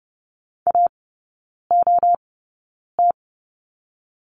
Morse code, heard as a keyed tone, about 30 wpm, spaced as for 9 wpm, 700 Hz. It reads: AOT
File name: 081 Lesson 14e - Number 2 - Three Characters 30wpm